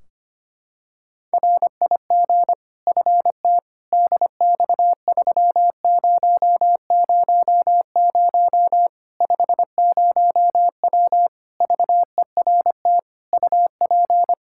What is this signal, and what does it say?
Morse code, heard as a keyed tone, about 25 wpm, 700 Hz. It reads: RIG FT DX3000 50W VERT UP
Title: Practice QSO - Exchange 2 - Info - 25wpm